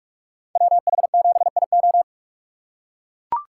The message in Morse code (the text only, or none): WH7IO